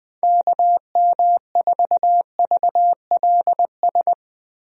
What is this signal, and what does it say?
Morse code, heard as a keyed tone, about 20 wpm, 700 Hz. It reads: KM4VLS
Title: USA Call Signs - 6 Characters - Repeated 3x - 20wpm